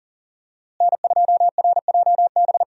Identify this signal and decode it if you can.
Morse code, heard as a keyed tone, about 40 wpm, 700 Hz. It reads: N2RJB
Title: USA Call Signs - 5 Characters - Repeated 2x - 40wpm